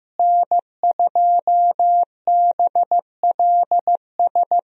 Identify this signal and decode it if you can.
Morse code, heard as a keyed tone, about 15 wpm, 700 Hz. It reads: N2BLS
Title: USA Call Signs - 5 Characters - Repeated 3x - 15wpm